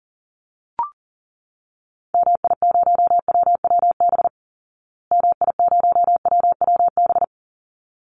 Morse code, heard as a keyed tone, about 40 wpm, 700 Hz. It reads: MI0WWB MI0WWB
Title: Call Signs Repeated 3x - 40wpm